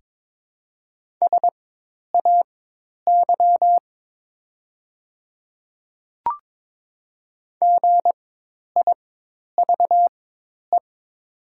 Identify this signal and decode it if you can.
Morse code, heard as a keyed tone, about 22 wpm, spaced as for 10 wpm, 700 Hz. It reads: SAY GIVE
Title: 4x Character Spacing - Top 200 Words - 22wpm